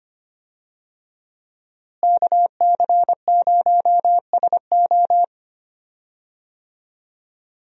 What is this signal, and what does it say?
Morse code, heard as a keyed tone, about 25 wpm, 700 Hz. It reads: KC0SO
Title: USA Call Signs - 5 Characters - Repeated 2x - 25wpm